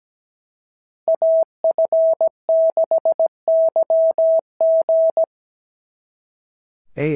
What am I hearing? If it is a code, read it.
Morse code, heard as a keyed tone, about 17 wpm, 650 Hz. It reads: AF6YG
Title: SOTA QSO Practice 17wpm